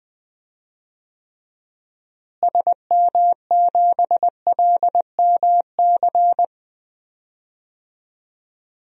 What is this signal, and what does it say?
Morse code, heard as a keyed tone, about 20 wpm, 700 Hz. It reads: SM7LMC
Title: DXCC-100 Call Signs - 6 Characters - Repeated 2x - 20wpm